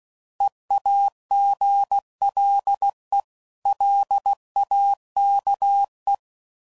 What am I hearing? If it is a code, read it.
Morse code, heard as a keyed tone, about 16 wpm, 800 Hz. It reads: EAGLE LAKE
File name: CW Code Groups @15WPM / fromSM's / Cities